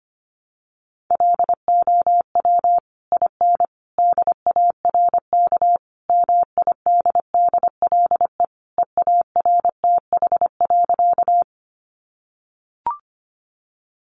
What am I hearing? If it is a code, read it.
Morse code, heard as a keyed tone, about 25 wpm, 700 Hz. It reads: LOW IN DARK MIDDLE EARTH.
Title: Sentences from Top 900 Words 25wpm